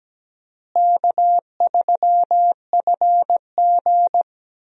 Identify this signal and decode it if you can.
Morse code, heard as a keyed tone, about 17 wpm, 700 Hz. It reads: K3FG
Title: USA Call Signs - 4 Characters - Repeated 2x - 17wpm